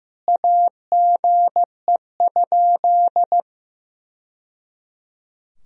Morse code, heard as a keyed tone, about 15 wpm, 700 Hz. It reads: AGE?
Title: Sentences from Top 1000 Words - Review Part 3 - 15wpm